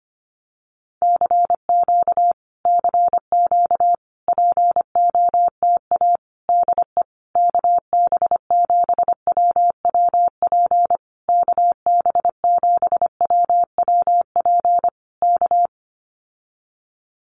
Morse code, heard as a keyed tone, about 25 wpm, 700 Hz. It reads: CQ CQ POTA DE KB7WWP KB7WWP K
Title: Common QSO Phrases Encore 25wpm